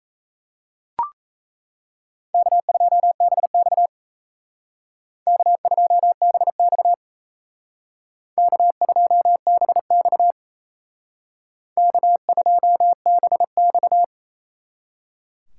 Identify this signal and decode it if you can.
Morse code, heard as a keyed tone, about 36 wpm, 700 Hz. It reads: K2BX K2BX K2BX K2BX